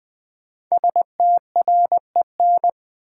Morse code, heard as a keyed tone, about 20 wpm, 700 Hz. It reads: STREN
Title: Build A Word - 5 Letters At A Time - 10-Letter Words - 20wpm